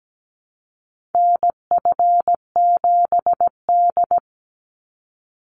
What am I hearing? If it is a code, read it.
Morse code, heard as a keyed tone, about 17 wpm, 700 Hz. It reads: NF7D
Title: Call Signs - 17wpm